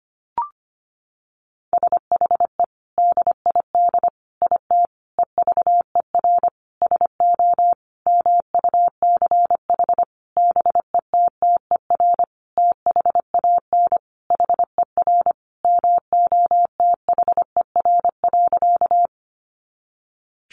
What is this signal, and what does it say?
Morse code, heard as a keyed tone, about 25 wpm, 700 Hz. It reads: SHE DID IT EVER SO MUCH BETTER THAN HER MOTHER.
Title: Sentences from Top 400 Words 25wpm